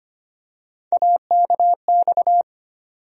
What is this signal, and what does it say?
Morse code, heard as a keyed tone, about 25 wpm, 700 Hz. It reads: AKX
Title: USA Call Sign Postfixes - 3 Letters 25wpm